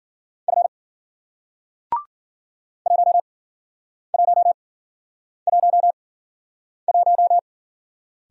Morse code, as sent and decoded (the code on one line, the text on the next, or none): ..-. / .---- .---- .---- .----
F 1111